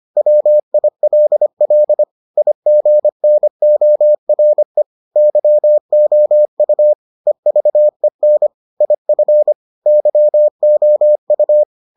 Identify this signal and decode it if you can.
Morse code, heard as a keyed tone, about 25 wpm, 600 Hz. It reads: WILL IGNORE YOU EVEN IF YOU